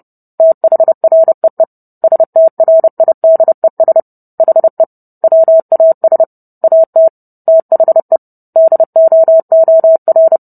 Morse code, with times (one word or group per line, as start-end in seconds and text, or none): 0.4-1.6 THREE
2.0-4.0 STRIDES
4.4-4.8 HE
5.2-6.2 WAS
6.6-7.1 AT
7.5-8.2 THE
8.6-10.4 DOOR